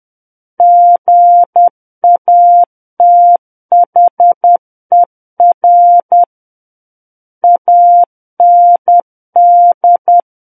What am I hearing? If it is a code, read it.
Morse code, heard as a keyed tone, about 10 wpm, 700 Hz. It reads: GATHER AND
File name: Book One / 01-Chapter 1 - 10 WPM